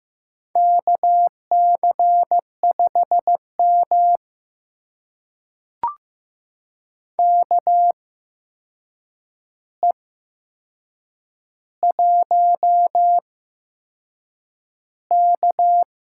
Morse code, heard as a keyed tone, about 15 wpm, spaced as for 6 wpm, 700 Hz. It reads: KC5M KE1K
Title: USA Call Signs - 4 Characters - 8x Character Spacing - 15wpm